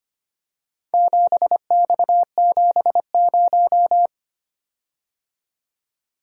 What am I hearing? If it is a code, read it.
Morse code, heard as a keyed tone, about 25 wpm, 700 Hz. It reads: 7X70
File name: practice / 25wpm / Four Letter-Number Combinations - 25wpm